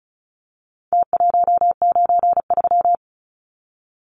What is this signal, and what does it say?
Morse code, heard as a keyed tone, about 34 wpm, 700 Hz. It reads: T193
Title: Numbers - Contest Style - 4-Digit 35wpm